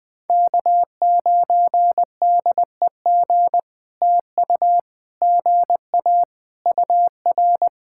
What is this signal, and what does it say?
Morse code, heard as a keyed tone, about 20 wpm, 700 Hz. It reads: K9DEG TU GA UR
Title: POTA QSO Practice - 20wpm